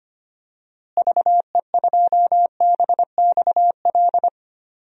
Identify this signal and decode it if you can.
Morse code, heard as a keyed tone, about 25 wpm, 700 Hz. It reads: VE2BXL
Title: DXCC-100 Call Signs - 6 Characters - Repeated 2x - 25wpm